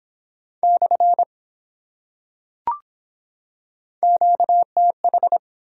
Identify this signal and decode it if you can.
Morse code, heard as a keyed tone, about 23 wpm, 700 Hz. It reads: / QTH